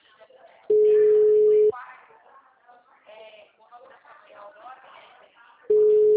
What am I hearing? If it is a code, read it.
Ringback tone (Europe, 425 Hz), with speech.
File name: out- -2012- - - .56158